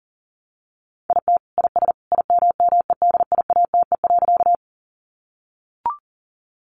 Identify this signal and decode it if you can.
Morse code, heard as a keyed tone, about 40 wpm, 700 Hz. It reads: IT IS IMMEDIATE.
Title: Sentences from Top 1200 Words 40wpm